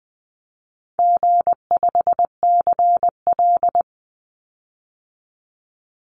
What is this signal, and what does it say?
Morse code, heard as a keyed tone, about 20 wpm, 700 Hz. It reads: G5CL
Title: Easy Call Signs - 20wpm